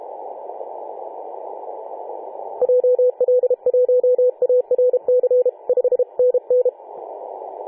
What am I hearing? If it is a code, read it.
Morse code, heard as a keyed tone, about 32 wpm, 500 Hz. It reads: JL1ARC 5NN
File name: 2013/Sep/21 2035z 3D2GC 18.072MHz CW